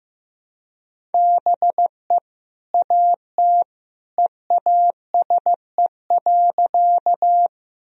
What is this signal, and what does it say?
Morse code, heard as a keyed tone, about 16 wpm, 700 Hz. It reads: BE AT EASE.
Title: Sentences from Top 1000 Words - Review Part 1 - 15wpm